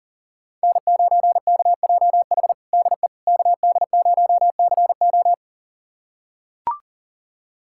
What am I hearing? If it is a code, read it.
Morse code, heard as a keyed tone, about 40 wpm, 700 Hz. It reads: N9KJH DE KD0CO
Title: USA Callsign DE Callsign - 5 Characters - 40wpm